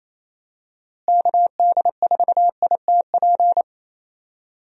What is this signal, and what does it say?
Morse code, heard as a keyed tone, about 27 wpm, 700 Hz. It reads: KD4ITP
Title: USA Call Signs - 6 Characters - Repeated 2x - 28wpm